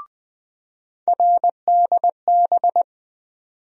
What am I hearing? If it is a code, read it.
Morse code, heard as a keyed tone, about 20 wpm, 700 Hz. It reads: RDB
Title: USA Call Sign Postfixes - 3 Letters 20wpm